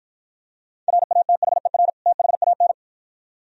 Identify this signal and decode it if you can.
Morse code, heard as a keyed tone, about 53 wpm, 700 Hz. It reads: RATHER THAN